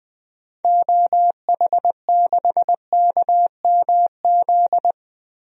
Morse code, heard as a keyed tone, about 20 wpm, 700 Hz. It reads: OH6KMZ
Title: DXCC-100 Call Signs - 6 Characters - Repeated 3x - 20wpm